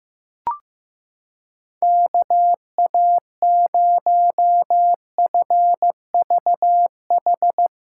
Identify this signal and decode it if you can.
Morse code, heard as a keyed tone, about 15 wpm, 700 Hz. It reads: KA0FVH
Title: USA Callsign DE Callsign - 6 Characters - 15wpm